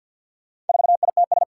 Morse code, heard as a keyed tone, about 53 wpm, 700 Hz. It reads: 4ITS
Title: Four Challenging Letter-Number Combinations 50wpm